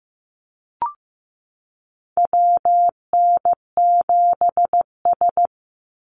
Morse code, heard as a keyed tone, about 15 wpm, 700 Hz. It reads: WN7S
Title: Call Signs Repeated 2x - 15wpm